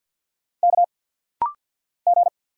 Morse code, heard as a keyed tone, about 50 wpm, 700 Hz. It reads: K G
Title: Single Letter-Number Alt - Rapid-Fire 50wpm